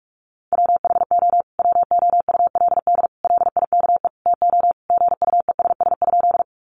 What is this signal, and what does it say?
Morse code, heard as a keyed tone, about 44 wpm, 700 Hz. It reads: WHO WOULD LIKE TO GUESS?
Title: Sentences from Top 1500 Words 45wpm